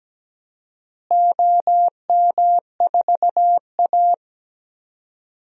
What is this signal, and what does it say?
Morse code, heard as a keyed tone, about 17 wpm, 700 Hz. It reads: OM4A
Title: DXCC-100 Call Signs - 4 Characters - Repeated 2x - 17wpm